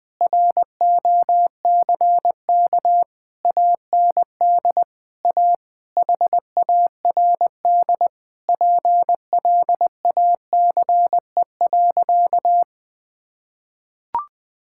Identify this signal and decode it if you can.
Morse code, heard as a keyed tone, about 20 wpm, 700 Hz. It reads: ROCK AND A HARD PLACE.
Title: Sentences with 8-Word Phrases - 20wpm